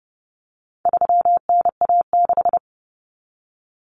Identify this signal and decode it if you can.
Morse code, heard as a keyed tone, about 30 wpm, 700 Hz. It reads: 3NA6
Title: Common Cut Numbers - 4-Digit 30wpm